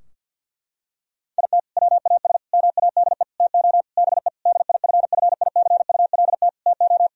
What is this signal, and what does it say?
Morse code, heard as a keyed tone, about 51 wpm, 700 Hz. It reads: IT WAS MADE TO BE DIFFICULT TO
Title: Sentences from Top 1000 Words - Review Part 4 - 50wpm